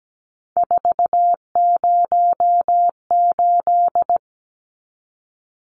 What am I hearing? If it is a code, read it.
Morse code, heard as a keyed tone, about 17 wpm, 700 Hz. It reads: 408
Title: Numbers - 3-Digit 17wpm